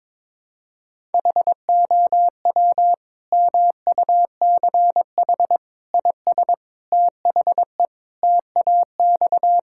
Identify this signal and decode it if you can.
Morse code, heard as a keyed tone, about 22 wpm, 700 Hz. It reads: HOW MUCH IS THE TAX
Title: Sentences from Top 1000 Words - Review Part 5 - 22wpm